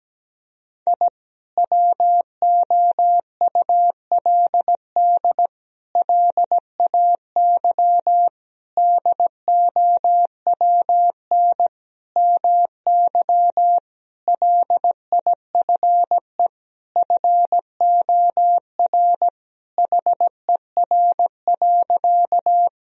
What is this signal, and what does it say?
Morse code, heard as a keyed tone, about 17 wpm, 700 Hz. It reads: I WOULD LAY DOWN MY LIFE FOR HER.
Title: Sentences from Top 1000 Words - Review Part 2 - 17wpm